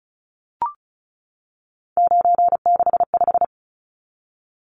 Morse code, heard as a keyed tone, about 34 wpm, 700 Hz. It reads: 965
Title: Numbers - 3-Digit 35wpm